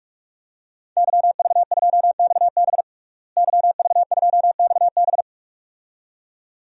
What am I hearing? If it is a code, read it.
Morse code, heard as a keyed tone, about 44 wpm, 700 Hz. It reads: YV2XB YV2XB
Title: DXCC-100 Call Signs - 5 Characters - Repeated 2x - 45wpm